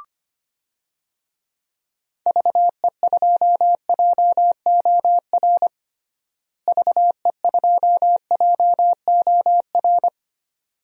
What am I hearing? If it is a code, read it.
Morse code, heard as a keyed tone, about 25 wpm, 700 Hz. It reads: VE2JOR VE2JOR
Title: DXCC-100 Call Signs - 6 Characters - Repeated 2x - 25wpm